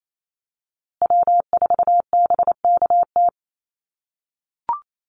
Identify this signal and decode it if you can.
Morse code, heard as a keyed tone, about 28 wpm, 700 Hz. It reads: W4BKT
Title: USA Call Signs Repeated 2x 28wpm